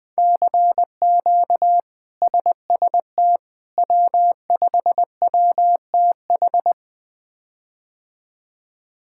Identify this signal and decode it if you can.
Morse code, heard as a keyed tone, about 20 wpm, 700 Hz. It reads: CQ SST W5WTH
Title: K1USN SST QSO Practice - 20wpm